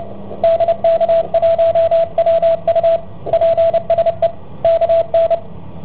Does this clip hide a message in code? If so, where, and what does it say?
Yes: Morse code from 0.4 to 5.3 s, heard as a keyed tone, about 29 wpm, 660 Hz, with radio noise. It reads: DK1WU PSE KN